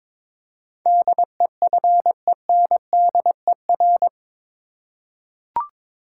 Morse code, heard as a keyed tone, about 21 wpm, 700 Hz. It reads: DEFENDER
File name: practice / 22wpm / Eight Letter Words - 22wpm